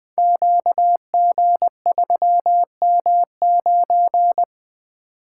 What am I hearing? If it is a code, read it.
Morse code, heard as a keyed tone, about 20 wpm, 700 Hz. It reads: QG3M9
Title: Code Groups - 5 Letter:Numbers - 20wpm